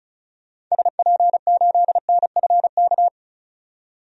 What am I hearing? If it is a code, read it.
Morse code, heard as a keyed tone, about 36 wpm, 700 Hz. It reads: SP8NFK
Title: DXCC-100 Call Signs - 6 Characters - 35wpm